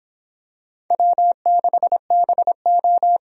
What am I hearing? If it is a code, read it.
Morse code, heard as a keyed tone, about 26 wpm, 700 Hz. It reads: W6BO